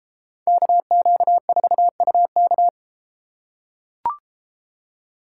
Morse code, heard as a keyed tone, about 33 wpm, 700 Hz. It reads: KQ4UK